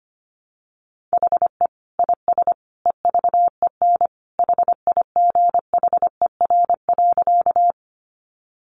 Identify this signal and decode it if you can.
Morse code, heard as a keyed tone, about 25 wpm, 700 Hz. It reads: HE IS EVEN HIGHER.
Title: Additional Easy Phrases 25wpm